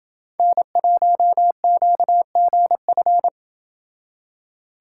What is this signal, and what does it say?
Morse code, heard as a keyed tone, about 28 wpm, 700 Hz. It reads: N1QGF